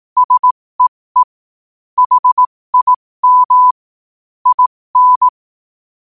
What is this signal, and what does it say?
Morse code, heard as a keyed tone, about 18 wpm, spaced as for 15 wpm, 1000 Hz. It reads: SEE HIM IN
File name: 15wpm